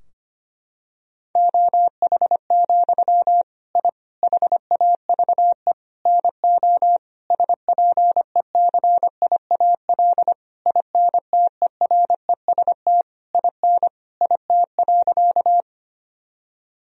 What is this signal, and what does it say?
Morse code, heard as a keyed tone, about 25 wpm, 700 Hz. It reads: OH, I HAVE NO SPECIAL INTEREST IN IT.
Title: Sentences from Top 1000 Words - Review Part 3 - 25wpm